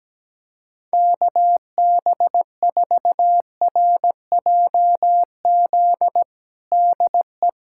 Morse code, heard as a keyed tone, about 17 wpm, 700 Hz. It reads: KB4RJZ DE
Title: USA Callsign DE Callsign - 6 Characters - 17wpm